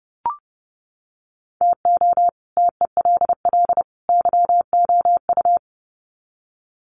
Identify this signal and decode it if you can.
Morse code, heard as a keyed tone, about 30 wpm, 700 Hz. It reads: TO TELL YOU
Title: Sets of 3 Words 30wpm